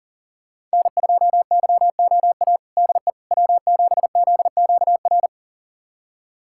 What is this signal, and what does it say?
Morse code, heard as a keyed tone, about 40 wpm, 700 Hz. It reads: N2YOA DE W7ZQR
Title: USA Callsign DE Callsign - 5 Characters - 40wpm